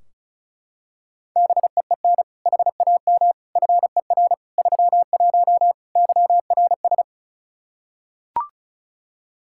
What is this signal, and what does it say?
Morse code, heard as a keyed tone, about 34 wpm, 700 Hz. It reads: BEEN HAM FER 31 YRS
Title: Practice QSO - Exchange 3 - More Info - 35wpm